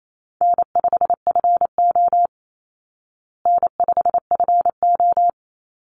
Morse code, heard as a keyed tone, about 28 wpm, 700 Hz. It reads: N5FO N5FO
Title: Call Signs Repeated 2x - 28wpm